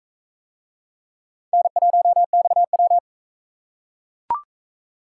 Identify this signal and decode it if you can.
Morse code, heard as a keyed tone, about 42 wpm, 700 Hz. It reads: N1XW